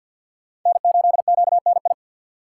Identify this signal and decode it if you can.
Morse code, heard as a keyed tone, about 60 wpm, 700 Hz. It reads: N8QNI
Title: Speed Racing - USA 1x3 Call Signs 50wpm